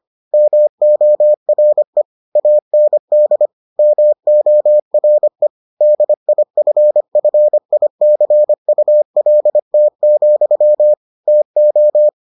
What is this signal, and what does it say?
Morse code, heard as a keyed tone, about 25 wpm, 600 Hz. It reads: MORE AND MORE DIFFICULT, TO